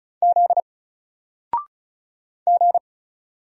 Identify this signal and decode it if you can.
Morse code, heard as a keyed tone, about 35 wpm, 700 Hz. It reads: Z G
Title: Single Character (Letter:Number:Punctuation:Prosign) 35wpm